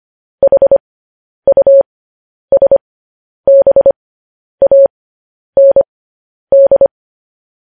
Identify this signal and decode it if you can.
Morse code, heard as a keyed tone, about 25 wpm, spaced as for 10 wpm, 550 Hz. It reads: HUSBAND
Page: news-headlines-in-morse-code-at-10-wpm